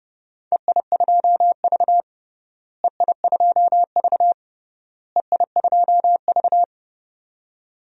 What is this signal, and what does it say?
Morse code, heard as a keyed tone, about 32 wpm, 700 Hz. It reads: EI2V EI2V EI2V
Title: DXCC-100 Call Signs - 4 Characters - Repeated 3x - 30wpm